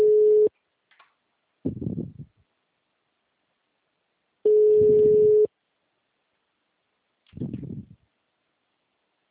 Ringback tone (Europe, 425 Hz).
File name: out- -2011- - - .497